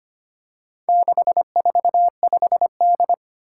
Morse code, heard as a keyed tone, about 25 wpm, 700 Hz. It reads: 645D
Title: Four Challenging Letter-Number Combinations 25wpm